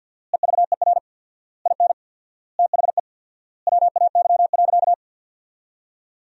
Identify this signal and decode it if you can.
Morse code, heard as a keyed tone, about 50 wpm, 700 Hz. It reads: EVER IN THE WAY.
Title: 4x Word-Spacing - Sentences from Top 400 Words - 50wpm